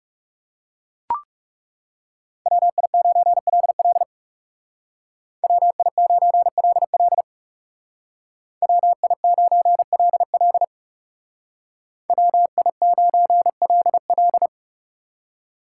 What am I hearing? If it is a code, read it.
Morse code, heard as a keyed tone, about 39 wpm, 700 Hz. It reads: WI9LL WI9LL WI9LL WI9LL